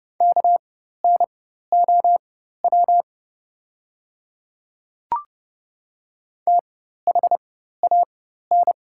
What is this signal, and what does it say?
Morse code, heard as a keyed tone, about 30 wpm, spaced as for 14 wpm, 690 Hz. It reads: KNOW THAN
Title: 4x Character Spacing - Top 100 Words - 30wpm